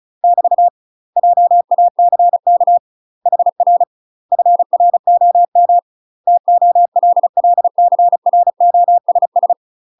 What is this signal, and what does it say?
Morse code, heard as a keyed tone, about 34 wpm, 700 Hz. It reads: <BT> JACK HR FROM TOLLCROSS
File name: UK MASTER QSOs TRAINING / 35WPM / UK MASTER QSOs 0043